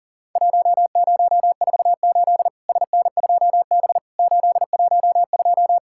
Morse code, heard as a keyed tone, about 42 wpm, 700 Hz. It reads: 1048 SN2B 812
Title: Worked All Europe DX Contest - QTC Exchange - 40wpm